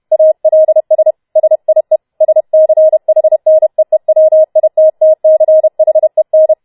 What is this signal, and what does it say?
Morse code, heard as a keyed tone, about 31 wpm, 630 Hz. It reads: ALS SIE SCHNEEWITTCHEN